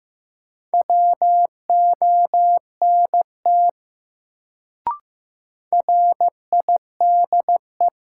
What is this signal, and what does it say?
Morse code, heard as a keyed tone, about 15 wpm, 700 Hz. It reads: WONT RIDE
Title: Sentences from Top 1000 Words - Review Part 6 - New Word Drill - 15wpm